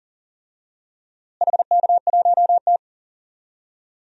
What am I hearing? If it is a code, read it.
Morse code, heard as a keyed tone, about 40 wpm, 700 Hz. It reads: HK1T
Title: DXCC-100 Call Signs - 4 Characters - Repeated 2x - 40wpm